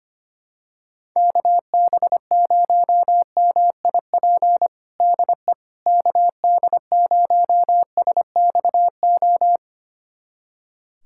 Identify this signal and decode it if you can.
Morse code, heard as a keyed tone, about 25 wpm, 700 Hz. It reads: KB0MIP DE KD0SXO
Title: USA Callsign DE Callsign 25wpm